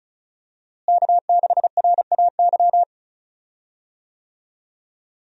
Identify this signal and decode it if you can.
Morse code, heard as a keyed tone, about 36 wpm, 700 Hz. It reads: K6RAY
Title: USA Call Signs - 5 Characters - Repeated 2x - 35wpm